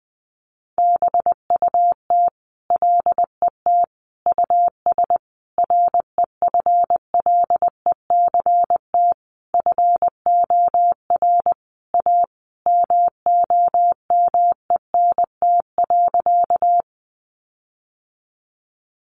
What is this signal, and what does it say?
Morse code, heard as a keyed tone, about 20 wpm, 700 Hz. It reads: BUT LET US REFLECT FOR A MOMENT.
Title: Sentences from Top 1500 Words 20wpm